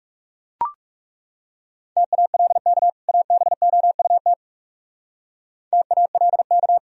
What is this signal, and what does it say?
Morse code, heard as a keyed tone, about 37 wpm, 700 Hz. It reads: TALK ABOUT TALK
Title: Speed-Racing - Sets of 2 Words 30wpm